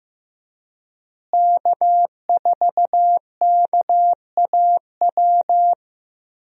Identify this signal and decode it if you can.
Morse code, heard as a keyed tone, about 15 wpm, 700 Hz. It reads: K4KAW
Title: USA Call Signs - 5 Characters - Repeated 2x - 15wpm